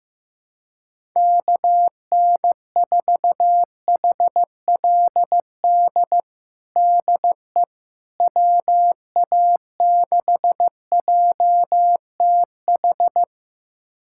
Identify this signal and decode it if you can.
Morse code, heard as a keyed tone, about 15 wpm, 700 Hz. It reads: KN4HLD DE WA6JTH
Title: USA Callsign DE Callsign 15wpm